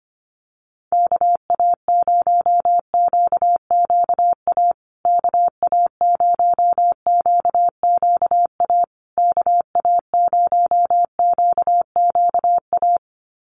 Morse code, heard as a keyed tone, about 25 wpm, 700 Hz. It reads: KA0QQA KA0QQA KA0QQA
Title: USA Call Signs Repeated 3x 25wpm